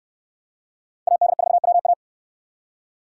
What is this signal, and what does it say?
Morse code, heard as a keyed tone, about 61 wpm, 700 Hz. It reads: AD4WA